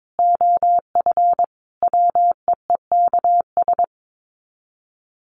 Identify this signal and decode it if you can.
Morse code, heard as a keyed tone, about 22 wpm, 700 Hz. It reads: OF WEEKS
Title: Sets of 5 Words Encore2 22wpm